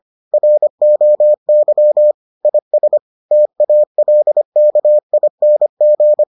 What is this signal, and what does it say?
Morse code, heard as a keyed tone, about 25 wpm, 600 Hz. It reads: ROY IS TALKING